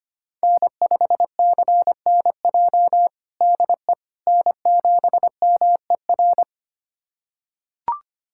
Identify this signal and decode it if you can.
Morse code, heard as a keyed tone, about 25 wpm, 700 Hz. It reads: N5CNJ DE N7MER
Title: USA Callsign DE Callsign - 5 Characters - 25wpm